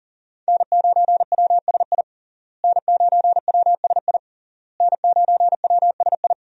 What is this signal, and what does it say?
Morse code, heard as a keyed tone, about 40 wpm, 700 Hz. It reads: N9WSI N9WSI N9WSI
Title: USA Call Signs - 5 Characters - Repeated 3x - 40wpm